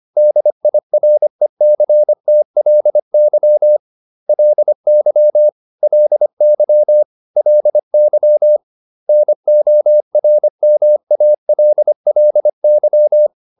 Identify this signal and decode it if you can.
Morse code, heard as a keyed tone, about 25 wpm, 600 Hz. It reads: DIRECTLY LY LY LY NORMALLY